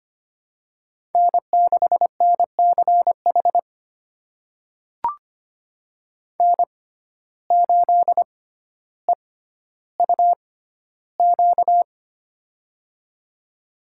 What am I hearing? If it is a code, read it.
Morse code, heard as a keyed tone, about 25 wpm, spaced as for 13 wpm, 700 Hz. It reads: N6NCH N8EUQ
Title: USA Call Signs - 5 Characters - 6x Character Spacing - 25wpm